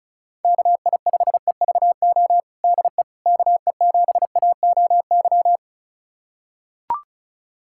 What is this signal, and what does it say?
Morse code, heard as a keyed tone, about 34 wpm, 700 Hz. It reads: KI5EVO DE KE7AOY
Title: USA Callsign DE Callsign 35wpm